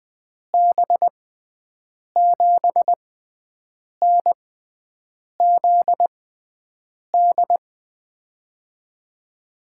Morse code, heard as a keyed tone, about 20 wpm, spaced as for 7 wpm, 700 Hz. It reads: B7NZD
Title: USA Call Signs - 6 Characters - 6x Character Spacing - 20wpm